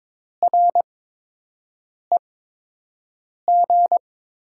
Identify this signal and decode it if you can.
Morse code, heard as a keyed tone, about 22 wpm, spaced as for 6 wpm, 700 Hz. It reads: REG
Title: USA Call Signs - 5 Characters - 8x Character Spacing - 22wpm